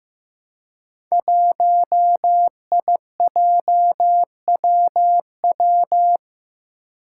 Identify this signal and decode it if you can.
Morse code, heard as a keyed tone, about 15 wpm, 700 Hz. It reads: 1IJWW
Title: Five Challenging Letter-Number Combinations 15wpm